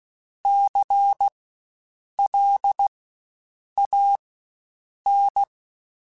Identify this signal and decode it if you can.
Morse code, heard as a keyed tone, about 16 wpm, spaced as for 7 wpm, 800 Hz. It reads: CLAN
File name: Spaced Alphas / Spaced Alphas mp3s / Cs sp @15